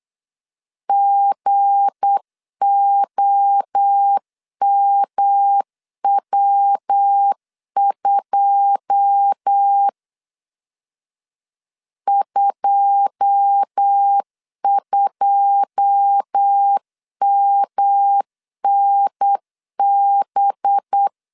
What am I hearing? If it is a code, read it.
Morse code, heard as a keyed tone, about 8 wpm, 800 Hz. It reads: GOMW2 22MNB
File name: LEZIONI / 9 2 8wpm 3437